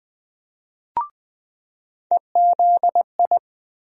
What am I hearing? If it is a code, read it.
Morse code, heard as a keyed tone, about 21 wpm, 700 Hz. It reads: EZI
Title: USA Call Sign Postfixes - 3 Letters 20wpm